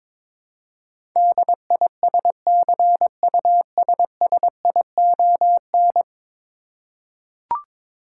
Morse code, heard as a keyed tone, about 22 wpm, 700 Hz. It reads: DISCUSSION
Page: morse-code-ninja-practice-22wpm